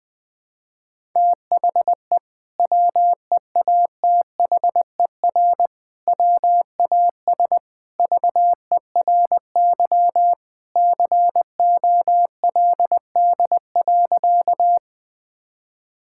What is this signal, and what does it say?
Morse code, heard as a keyed tone, about 19 wpm, 700 Hz. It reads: THE WEATHER WAS VERY COLD.
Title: Sentences from Top 1000 Words - Review Part 5 - 20wpm